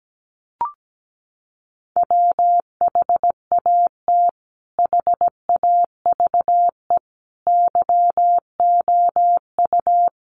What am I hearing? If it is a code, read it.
Morse code, heard as a keyed tone, about 17 wpm, 700 Hz. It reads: WHAT HAVE YOU
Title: Sentences from Top 400 Words 17wpm